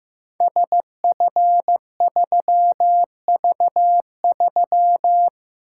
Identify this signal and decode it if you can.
Morse code, heard as a keyed tone, about 15 wpm, 700 Hz. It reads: SF3V3
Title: Code Groups - 5 Letter:Numbers - 15wpm